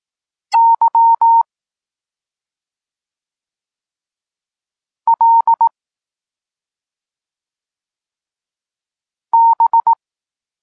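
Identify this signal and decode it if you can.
Morse code, heard as a keyed tone, about 18 wpm, 920 Hz. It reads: Y L B
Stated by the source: Polish Morse lesson teaching the letter C, from SP9MOA Niepołomicki Klub Krótkofalowców.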